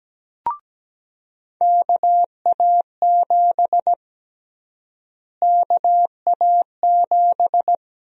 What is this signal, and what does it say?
Morse code, heard as a keyed tone, about 17 wpm, 700 Hz. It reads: KA7 KA7
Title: USA Call Sign Prefixes - Repeated 2x - 17wpm